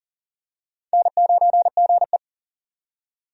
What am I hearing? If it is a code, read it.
Morse code, heard as a keyed tone, about 40 wpm, 700 Hz. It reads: N9GE
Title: USA Call Signs - 4 Characters - 40wpm